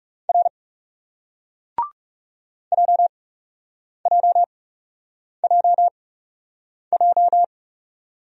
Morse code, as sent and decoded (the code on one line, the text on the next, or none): .-. / .--- .--- .--- .---
R JJJJ